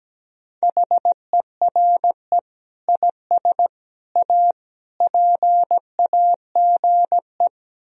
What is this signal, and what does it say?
Morse code, heard as a keyed tone, about 17 wpm, 700 Hz. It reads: HERE IS A PAGE
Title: Sentences from Top 1000 Words - Review Part 1 - 17wpm